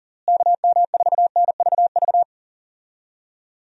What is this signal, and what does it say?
Morse code, heard as a keyed tone, about 40 wpm, 700 Hz. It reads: KM4NVV
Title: USA Call Signs - 6 Characters - Repeated 2x - 40wpm